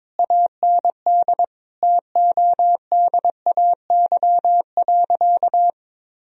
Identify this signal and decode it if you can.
Morse code, heard as a keyed tone, about 22 wpm, 700 Hz. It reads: AND TODAY.
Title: Sentences from Top 1000 Words - Review Part 4 - 22wpm